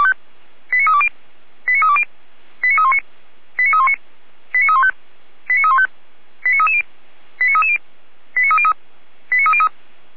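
Five-tone selective call (ZVEI), heard as quick runs of five tones, tones of about 80 ms. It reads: 89320 89320 89319 89319 89315 89315 89330 89330 89303 89303
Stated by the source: ZVEI five-tone selective call 89110 received on an SDR.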